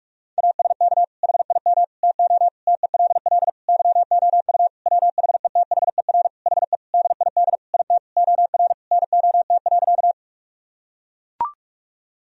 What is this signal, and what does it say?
Morse code, heard as a keyed tone, about 44 wpm, 700 Hz. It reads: ASK HIM TO TELL YOU WHETHER HE DID IT OR NOT.
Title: Sentences from Top 1000 Words - Review Part 3 - 45wpm